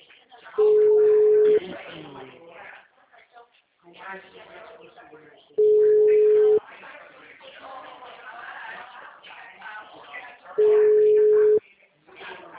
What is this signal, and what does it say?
Ringback tone (Europe, 425 Hz), with speech.
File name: out- -2001- - - .37132